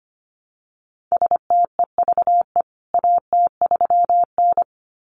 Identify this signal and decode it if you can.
Morse code, heard as a keyed tone, about 25 wpm, 700 Hz. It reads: STEVE AT3N
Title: CWT Exchange 25wpm